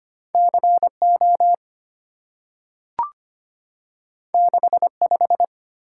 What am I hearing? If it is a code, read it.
Morse code, heard as a keyed tone, about 25 wpm, 690 Hz. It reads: CO 65
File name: Two Letter-Number Combinations - 25wpm